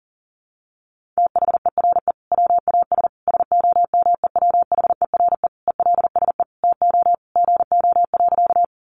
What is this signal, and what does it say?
Morse code, heard as a keyed tone, about 40 wpm, 700 Hz. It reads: THERE WAS SOMEWHERE ELSE TO GO.
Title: Sentences from Top 1400 Words 40wpm